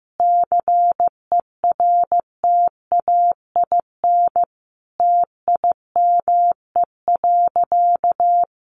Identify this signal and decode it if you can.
Morse code, heard as a keyed tone, about 15 wpm, 700 Hz. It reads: CERTAIN TIME.
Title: Sentences from Top 1000 Words 15wpm